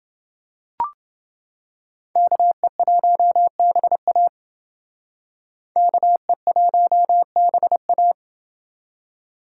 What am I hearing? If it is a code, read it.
Morse code, heard as a keyed tone, about 28 wpm, 700 Hz. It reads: KE1BA KE1BA